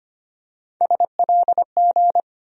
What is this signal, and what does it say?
Morse code, heard as a keyed tone, about 25 wpm, 700 Hz. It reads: SLG
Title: USA Call Sign Postfixes - 3 Letters 25wpm